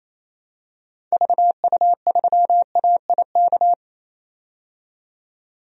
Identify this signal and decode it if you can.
Morse code, heard as a keyed tone, about 28 wpm, 700 Hz. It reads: VU3AIK
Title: DXCC-100 Call Signs - 6 Characters - 28wpm